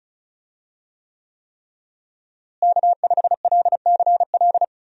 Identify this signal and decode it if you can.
Morse code, heard as a keyed tone, about 36 wpm, 700 Hz. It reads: K5LCL
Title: USA Call Signs - 5 Characters - Repeated 3x - 35wpm